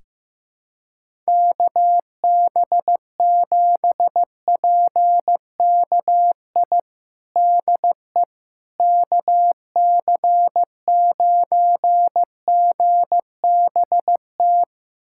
Morse code, heard as a keyed tone, about 15 wpm, 700 Hz. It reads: KB7PKI DE KC9GBT
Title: USA Callsign DE Callsign - 6 Characters - 15wpm